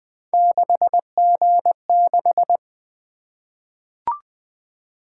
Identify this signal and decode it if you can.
Morse code, heard as a keyed tone, about 20 wpm, 700 Hz. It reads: 6G6
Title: Three Letter-Number Combinations - 20wpm